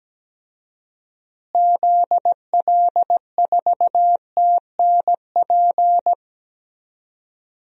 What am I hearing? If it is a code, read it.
Morse code, heard as a keyed tone, about 17 wpm, 700 Hz. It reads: ZL4TNP
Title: DXCC-100 Call Signs - 6 Characters - Repeated 3x - 17wpm